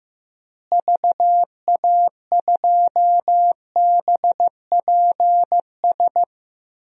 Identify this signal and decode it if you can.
Morse code, heard as a keyed tone, about 15 wpm, 700 Hz. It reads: VA2BPS
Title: DXCC-100 Call Signs - 6 Characters - Repeated 2x - 15wpm